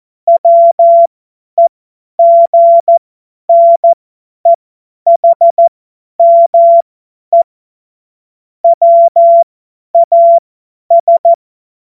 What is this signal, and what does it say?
Morse code, heard as a keyed tone, about 14 wpm, spaced as for 10 wpm, 680 Hz. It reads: WEGNEHME WAS
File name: Morsekurs / Witze / Witz 17. August 2021, 50 F70